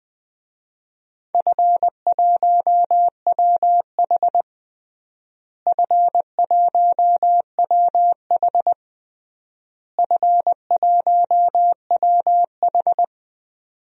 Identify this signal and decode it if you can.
Morse code, heard as a keyed tone, about 20 wpm, 700 Hz. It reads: F1WH F1WH F1WH
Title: DXCC-100 Call Signs - 4 Characters - Repeated 3x - 20wpm